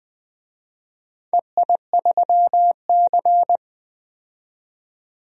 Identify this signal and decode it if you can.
Morse code, heard as a keyed tone, about 20 wpm, 700 Hz. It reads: EI3C
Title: DXCC-100 Call Signs - 4 Characters - 20wpm